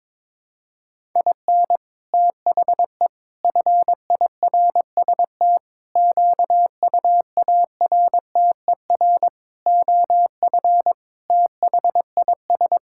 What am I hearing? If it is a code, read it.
Morse code, heard as a keyed tone, about 22 wpm, 700 Hz. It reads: IN THE FIRST QUARTER OF THIS
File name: Sentences with 7-Word Phrases - 22wpm